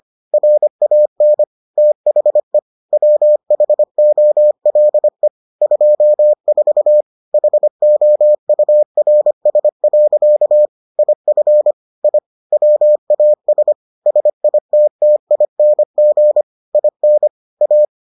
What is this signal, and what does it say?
Morse code, heard as a keyed tone, about 25 wpm, 600 Hz. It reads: RAN THE WHOLE 24 HOURS. IF I WAS SITTING IN A